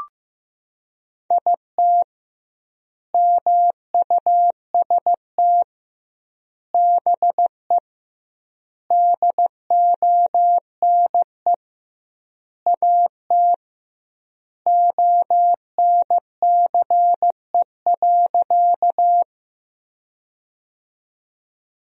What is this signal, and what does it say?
Morse code, heard as a keyed tone, about 15 wpm, 700 Hz. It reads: IT MUST BE DONE AT ONCE.
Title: 2x Word-Spacing - Sentences from Top 500 Words - 15wpm